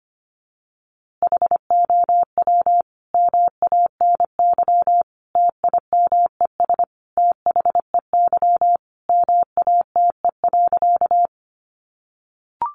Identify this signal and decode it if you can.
Morse code, heard as a keyed tone, about 25 wpm, 700 Hz. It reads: HOW MANY TIMES THEY MATE.
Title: A Sound Of Thunder 25wpm